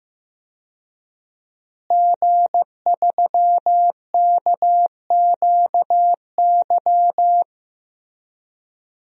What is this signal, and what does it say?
Morse code, heard as a keyed tone, about 15 wpm, 700 Hz. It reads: G3KQY
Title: DXCC-100 Call Signs - 5 Characters - Repeated 3x - 15wpm